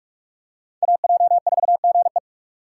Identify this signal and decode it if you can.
Morse code, heard as a keyed tone, about 43 wpm, 700 Hz. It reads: AJ4GE
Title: USA Call Signs - 4x Character Spacing - 45wpm